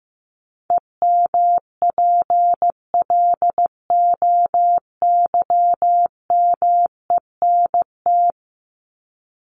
Morse code, heard as a keyed tone, about 15 wpm, 700 Hz. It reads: EMPLOYMENT
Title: Top 1251-1500 Words 15wpm